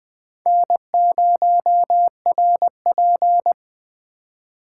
Morse code, heard as a keyed tone, about 20 wpm, 700 Hz. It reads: N0RP
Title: USA Call Signs - 4 Characters - 8x Character Spacing - 20wpm